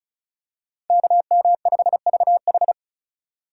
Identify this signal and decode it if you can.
Morse code, heard as a keyed tone, about 36 wpm, 700 Hz. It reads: KM5VH
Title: USA Call Signs - 5 Characters - 4x Character Spacing - 35wpm